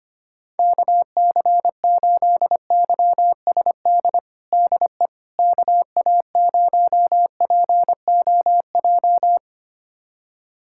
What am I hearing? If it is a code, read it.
Morse code, heard as a keyed tone, about 25 wpm, 700 Hz. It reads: KC8YSD DE KA0POJ
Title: USA Callsign DE Callsign - 6 Characters - 25wpm